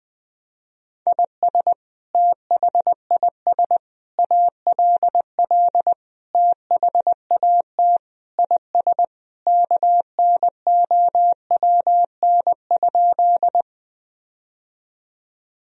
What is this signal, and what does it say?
Morse code, heard as a keyed tone, about 20 wpm, 700 Hz. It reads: IS THIS ALL THAT IS KNOWN?
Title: Sentences from Top 1000 Words - Review Part 3 - 20wpm